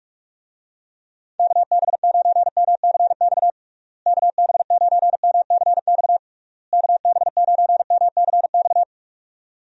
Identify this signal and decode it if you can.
Morse code, heard as a keyed tone, about 46 wpm, 700 Hz. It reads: KB9MCX KB9MCX KB9MCX
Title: USA Call Signs - 6 Characters - Repeated 3x - 45wpm